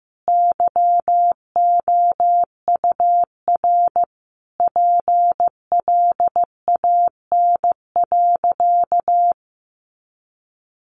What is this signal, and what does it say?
Morse code, heard as a keyed tone, about 15 wpm, 700 Hz. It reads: YOUR PLAN.
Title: Sentences from Top 500 Words 15wpm